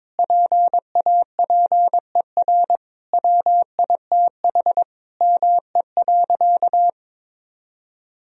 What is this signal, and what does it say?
Morse code, heard as a keyed tone, about 22 wpm, 700 Hz. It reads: PAPER WITH ME.
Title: Sentences from Top 1000 Words - Review Part 2 - 22wpm